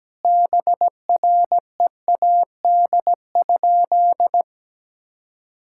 Morse code, heard as a keyed tone, about 17 wpm, 700 Hz. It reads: BREAD?
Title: Sentences from Top 1600 Words 17wpm